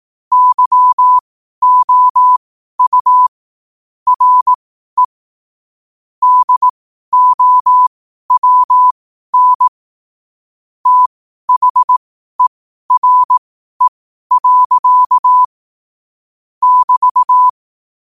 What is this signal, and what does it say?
Morse code, heard as a keyed tone, about 18 wpm, spaced as for 12 wpm, 1000 Hz. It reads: YOU RE DOWN THERE. <BT>
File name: qotdfiles / 12wpm